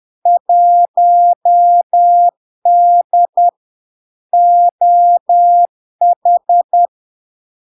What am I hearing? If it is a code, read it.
Morse code, heard as a keyed tone, about 10 wpm, 700 Hz. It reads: 1D OH
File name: LARK-FD-2022-Log-in-CW-10wpm-700-Hz